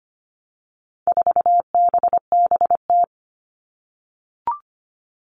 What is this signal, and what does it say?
Morse code, heard as a keyed tone, about 25 wpm, 700 Hz. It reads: 4BBT
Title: All Cut Numbers - 4-Digit 25wpm